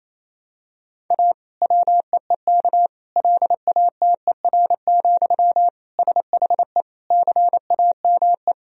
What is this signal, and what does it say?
Morse code, heard as a keyed tone, about 28 wpm, 700 Hz. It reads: A WEEK LATER, SHE CAME
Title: Sentences from Top 1000 Words - Review Part 4 - 28wpm